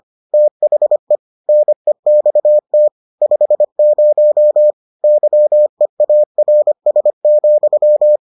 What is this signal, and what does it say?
Morse code, heard as a keyed tone, about 25 wpm, 600 Hz. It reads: THE NEXT 50 YEARS,